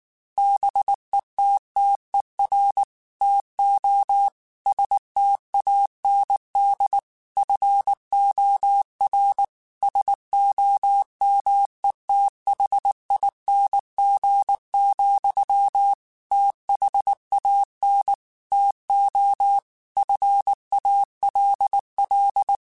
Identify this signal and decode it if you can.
Morse code, heard as a keyed tone, about 19 wpm, 800 Hz. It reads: BETTER TO STAND FOR SOMETHING, THAN TO FALL
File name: Quotes / Quotes-mp3s / StandFor @18